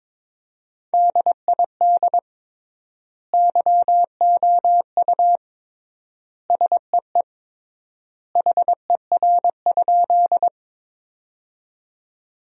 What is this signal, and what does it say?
Morse code, heard as a keyed tone, about 21 wpm, 700 Hz. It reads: DID YOU SEE HER?
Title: 3x Word-Spacing - Sentences from Top 100 Words - 22wpm